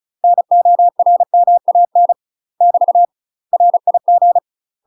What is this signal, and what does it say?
Morse code, heard as a keyed tone, about 34 wpm, 700 Hz. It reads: NORMAN <BT> RIG
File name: UK MASTER QSOs TRAINING / 35WPM / UK MASTER QSOs 0042